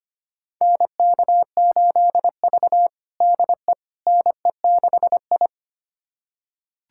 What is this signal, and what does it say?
Morse code, heard as a keyed tone, about 25 wpm, 700 Hz. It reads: NK8V DE NE6I
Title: USA Callsign DE Callsign - 4 Characters - 25wpm